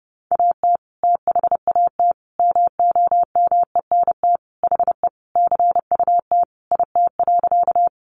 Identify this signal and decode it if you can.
Morse code, heard as a keyed tone, about 30 wpm, 700 Hz. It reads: AT THAT MOMENT HE CUT IT.
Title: Sentences from Top 1000 Words 30wpm